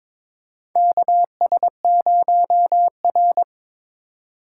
Morse code, heard as a keyed tone, about 21 wpm, 690 Hz. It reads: KS0R
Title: USA Call Signs - 4 Characters - 3x Character Spacing - 22wpm